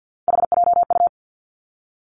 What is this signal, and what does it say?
Morse code, heard as a keyed tone, about 50 wpm, 700 Hz. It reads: HJU
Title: Three Challenging Letter-Number Combinations 50wpm